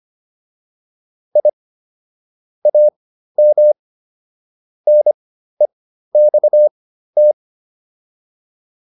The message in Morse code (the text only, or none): I AM NEXT